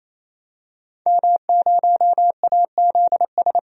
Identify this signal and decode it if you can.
Morse code, heard as a keyed tone, about 27 wpm, 700 Hz. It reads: M0AZS
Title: DXCC-100 Call Signs - 5 Characters - 28wpm